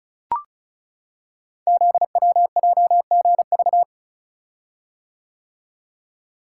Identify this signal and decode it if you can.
Morse code, heard as a keyed tone, about 36 wpm, 690 Hz. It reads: ZWJGV
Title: Code Groups - 5 Letters - 35wpm